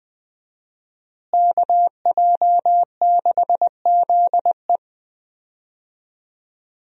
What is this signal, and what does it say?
Morse code, heard as a keyed tone, about 20 wpm, 700 Hz. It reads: KJ6ZE
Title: USA Call Signs - 5 Characters - Repeated 3x - 20wpm